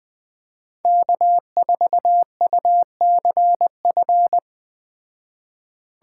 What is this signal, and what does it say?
Morse code, heard as a keyed tone, about 20 wpm, 700 Hz. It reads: K4UCF
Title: Speed Racing - USA 1x3 Call Signs 17wpm